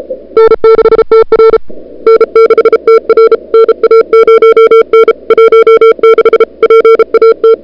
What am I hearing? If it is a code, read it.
Morse code, heard as a keyed tone, about 37 wpm, 440 Hz. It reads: N6TR N6TR NA0N 16 PAT
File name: CWSprint / February 2007 CW Sprint / 01-2007 CW Sprint Hour1